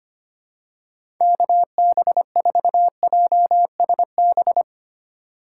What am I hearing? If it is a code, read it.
Morse code, heard as a keyed tone, about 25 wpm, 690 Hz. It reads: KB4JSB
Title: USA Call Signs - 6 Characters - 2x Character Spacing - 25wpm